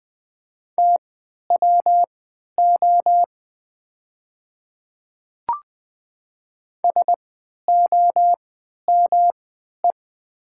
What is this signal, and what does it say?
Morse code, heard as a keyed tone, about 20 wpm, spaced as for 11 wpm, 700 Hz. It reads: TWO SOME
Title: 3x Character Spacing - Top 100 Words - 20wpm